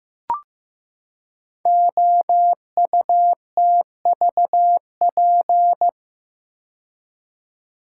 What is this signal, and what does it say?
Morse code, heard as a keyed tone, about 15 wpm, 700 Hz. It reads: OUTVP
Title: Code Groups - 5 Letter:Numbers - 15wpm